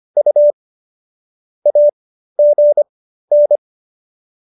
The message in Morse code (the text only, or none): U AGN